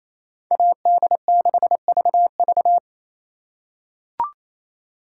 Morse code, heard as a keyed tone, about 27 wpm, 700 Hz. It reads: AD6VV
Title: USA Call Signs - 5 Characters - 4x Character Spacing - 28wpm